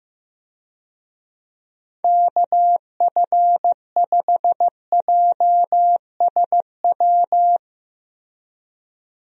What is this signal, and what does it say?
Morse code, heard as a keyed tone, about 15 wpm, 700 Hz. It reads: KF5JSW
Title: USA Call Signs - 6 Characters - Repeated 3x - 15wpm